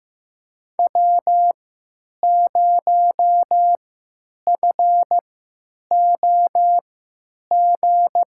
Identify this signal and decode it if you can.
Morse code, heard as a keyed tone, about 15 wpm, spaced as for 8 wpm, 700 Hz. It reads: W0FOG
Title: USA Call Signs - 5 Characters - 3x Character Spacing - 15wpm